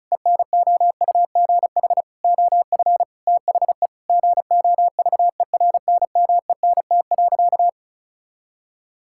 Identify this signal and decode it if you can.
Morse code, heard as a keyed tone, about 36 wpm, 700 Hz. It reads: ENOUGH OF THE GOVERNMENT.
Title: Sentences from Top 1600 Words 35wpm